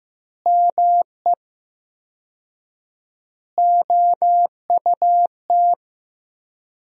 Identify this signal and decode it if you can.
Morse code, heard as a keyed tone, about 15 wpm, 700 Hz. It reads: ME OUT
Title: 4x Word-Spacing - Sentences from Top 400 Words - 15wpm